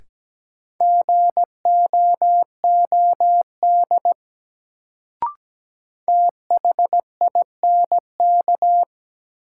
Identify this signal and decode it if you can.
Morse code, heard as a keyed tone, about 17 wpm, 700 Hz. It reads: GOOD THINK
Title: Sentences from Top 200 Words - New Word Drill - 17wpm